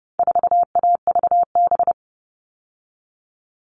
Morse code, heard as a keyed tone, about 30 wpm, 700 Hz. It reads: 4AVB
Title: All Cut Numbers - 4-Digit 30wpm